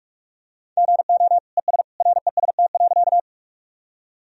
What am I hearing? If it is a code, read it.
Morse code, heard as a keyed tone, about 46 wpm, 690 Hz. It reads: GO ES REST.
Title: Sentences from Top 500 Words and Abbreviations 45wpm